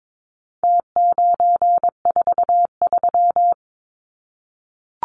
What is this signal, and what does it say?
Morse code, heard as a keyed tone, about 22 wpm, 700 Hz. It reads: T943
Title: Numbers - Contest Style - 4-Digit 22wpm